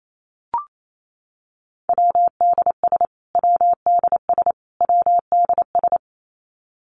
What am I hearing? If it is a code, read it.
Morse code, heard as a keyed tone, about 28 wpm, 700 Hz. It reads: WDS WDS WDS
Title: Abbreviations Repeated 3x - 28wpm